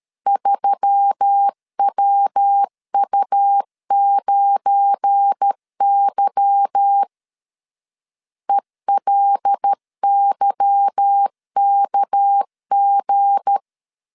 Morse code, heard as a keyed tone, about 13 wpm, 800 Hz. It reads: 3WU9Y ELYKG
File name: LEZIONI / 8 YJ45 / 8 Y 12wpm 2228